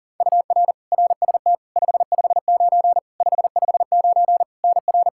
Morse code, heard as a keyed tone, about 40 wpm, 700 Hz. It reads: UR RST 559 559 NR
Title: Practice QSO - Exchange 1 - Beginning - 40wpm